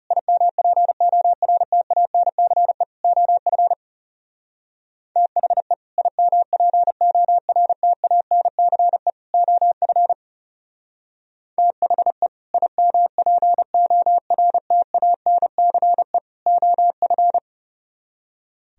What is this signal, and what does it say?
Morse code, heard as a keyed tone, about 36 wpm, 690 Hz. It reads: IMPORTANCE OF THE IMPORTANCE OF THE IMPORTANCE OF
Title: Speed-Racing - Sets of 3 Words 30wpm